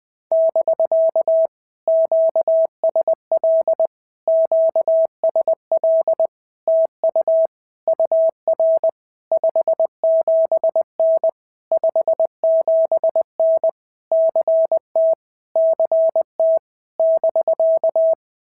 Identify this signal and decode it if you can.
Morse code, heard as a keyed tone, about 21 wpm, 650 Hz. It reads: <BK> QSL QSL TU UR 57N 57N CT CT <BK>
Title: POTA QSO Practice - 20wpm